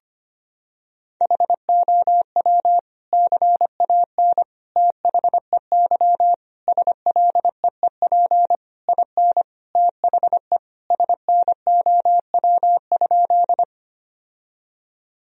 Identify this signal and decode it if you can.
Morse code, heard as a keyed tone, about 25 wpm, 700 Hz. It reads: HOW CAN THEY SLEEP IN THE SNOW?
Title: Sentences from Top 1000 Words - Review Part 3 - 25wpm